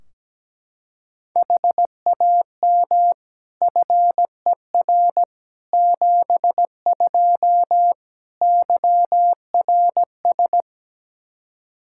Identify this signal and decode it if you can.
Morse code, heard as a keyed tone, about 17 wpm, 700 Hz. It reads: HAM FER 72 YRS
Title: Practice QSO - Exchange 3 - More Info - 17wpm